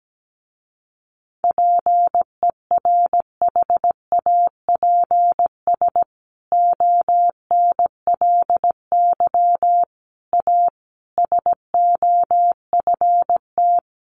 Morse code, heard as a keyed tone, about 17 wpm, 700 Hz. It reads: PERHAPS ONLY A SOFT
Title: A Sound Of Thunder 17wpm